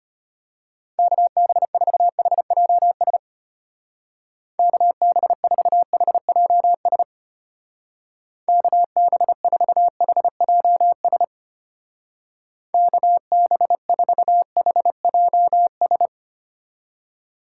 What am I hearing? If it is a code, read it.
Morse code, heard as a keyed tone, about 32 wpm, 700 Hz. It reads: KB4HJS KB4HJS KB4HJS KB4HJS